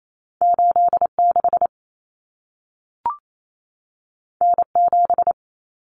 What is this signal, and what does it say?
Morse code, heard as a keyed tone, about 28 wpm, 700 Hz. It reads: 86 N7
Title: Common Cut Numbers - 2-Digit 28wpm